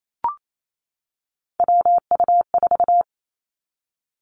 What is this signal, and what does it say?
Morse code, heard as a keyed tone, about 28 wpm, 700 Hz. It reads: WU4
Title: Three Challenging Letter-Number Combinations 28wpm